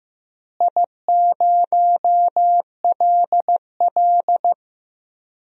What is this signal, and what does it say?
Morse code, heard as a keyed tone, about 15 wpm, 700 Hz. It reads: I0LL
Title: DXCC-100 Call Signs - 4 Characters - Repeated 2x - 15wpm